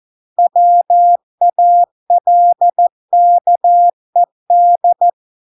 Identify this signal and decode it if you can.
Morse code, heard as a keyed tone, about 14 wpm, 700 Hz. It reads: WALKED